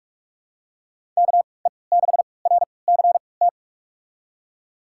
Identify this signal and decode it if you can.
Morse code, heard as a keyed tone, about 46 wpm, spaced as for 26 wpm, 700 Hz. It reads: KE6RCT
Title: USA Call Signs - 3x Character Spacing - 45wpm